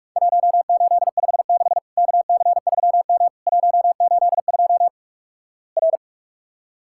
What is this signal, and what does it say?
Morse code, heard as a keyed tone, about 46 wpm, 700 Hz. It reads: 1856 KC3M 182 R
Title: Worked All Europe DX Contest - QTC Exchange - 45wpm